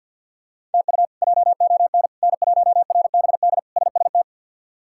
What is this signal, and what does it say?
Morse code, heard as a keyed tone, about 51 wpm, 700 Hz. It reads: TU JON N1RBD SST
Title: K1USN SST QSO Practice - 50wpm